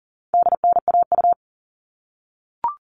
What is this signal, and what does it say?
Morse code, heard as a keyed tone, about 40 wpm, 700 Hz. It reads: DNAU
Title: All Cut Numbers - 4-Digit 40wpm